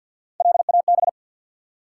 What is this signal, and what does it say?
Morse code, heard as a keyed tone, about 51 wpm, 700 Hz. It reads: LAB
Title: Three Letter Words - 50wpm